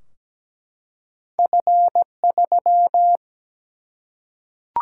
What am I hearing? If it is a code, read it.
Morse code, heard as a keyed tone, about 17 wpm, 700 Hz. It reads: F3
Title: Single-Letter Single-Number 17wpm